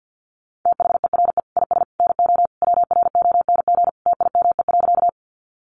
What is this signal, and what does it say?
Morse code, heard as a keyed tone, about 50 wpm, 700 Hz. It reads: THERE IS NO WRONG TIME.
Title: Sentences from Top 900 Words 50wpm